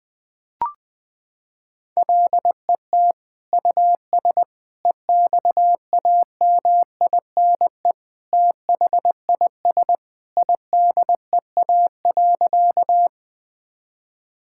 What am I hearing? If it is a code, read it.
Morse code, heard as a keyed tone, about 20 wpm, 700 Hz. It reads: LET US EXAMINE THIS IDEA.
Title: Sentences from Top 1600 Words 20wpm